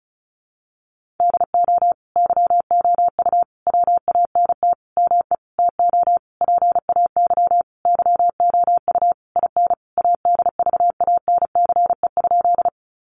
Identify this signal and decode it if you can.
Morse code, heard as a keyed tone, about 34 wpm, 700 Hz. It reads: DO YOU WANT ME TO PAY YOU IN ADVANCE?
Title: Sentences from Top 1400 Words 35wpm